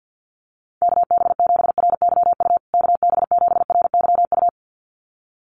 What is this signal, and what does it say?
Morse code, heard as a keyed tone, about 50 wpm, 700 Hz. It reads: KB7RYU KB7RYU
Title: USA Call Signs Repeated 2x 50wpm